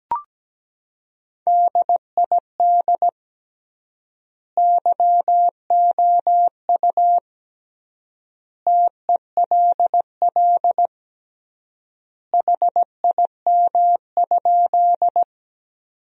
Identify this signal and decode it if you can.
Morse code, heard as a keyed tone, about 17 wpm, 700 Hz. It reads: DID YOU TELL HIM?
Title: 3x Word-Spacing - Sentences from Top 200 Words - 17wpm